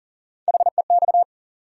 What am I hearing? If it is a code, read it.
Morse code, heard as a keyed tone, about 40 wpm, 700 Hz. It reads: HEX
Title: USA Call Sign Postfixes - 3 Letters 40wpm